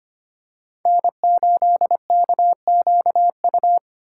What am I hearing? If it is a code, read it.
Morse code, heard as a keyed tone, about 25 wpm, 700 Hz. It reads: N8KQU
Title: USA Call Signs - 5 Characters - 2x Character Spacing - 25wpm